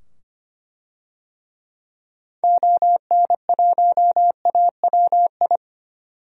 Morse code, heard as a keyed tone, about 25 wpm, 700 Hz. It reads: ON1AWI
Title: DXCC-100 Call Signs - 6 Characters - 25wpm